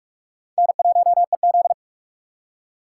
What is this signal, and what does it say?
Morse code, heard as a keyed tone, about 44 wpm, 690 Hz. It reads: N1EZ
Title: USA Call Signs - 4x Character Spacing - 45wpm